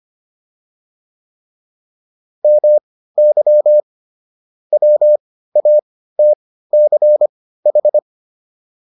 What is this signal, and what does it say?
Morse code, heard as a keyed tone, about 25 wpm, spaced as for 15 wpm, 600 Hz. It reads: MY WATCH